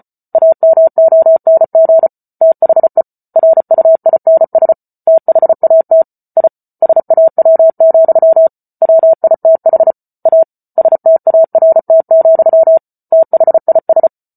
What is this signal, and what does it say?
Morse code, heard as a keyed tone, about 34 wpm, 650 Hz. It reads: AMONG THE RUINS THAT I SAW, WITH A START, THIS